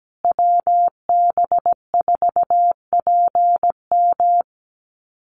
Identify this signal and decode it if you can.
Morse code, heard as a keyed tone, about 17 wpm, 700 Hz. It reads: WB4PM
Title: USA Call Signs Repeated 2x 17wpm